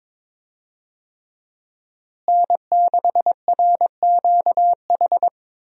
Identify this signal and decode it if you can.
Morse code, heard as a keyed tone, about 22 wpm, 700 Hz. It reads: N6RQH
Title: USA Call Signs - 5 Characters - Repeated 3x - 22wpm